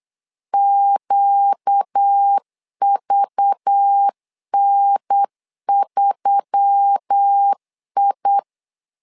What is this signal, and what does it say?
Morse code, heard as a keyed tone, about 8 wpm, 800 Hz. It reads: QVN3I
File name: LEZIONI / 5 LBVC / 5 LBV 8wpm 2837